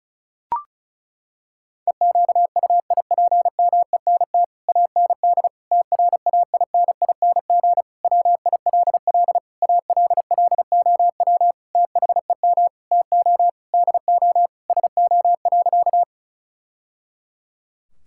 Morse code, heard as a keyed tone, about 34 wpm, 700 Hz. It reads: EQUIPMENT AND TRAINING WILL ALLOW THEM TO DO SO.
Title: Sentences from Top 1000 Words - Review Part 5 - 35wpm